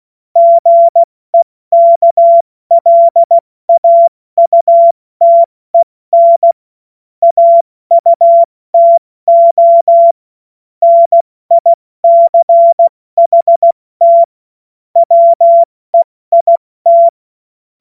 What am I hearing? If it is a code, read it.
Morse code, heard as a keyed tone, about 16 wpm, 680 Hz. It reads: GEKLAUTEN AUTO NICHT WEIT
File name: Morsekurs / Witze / Witz 28. Dezember 2021, 70 F80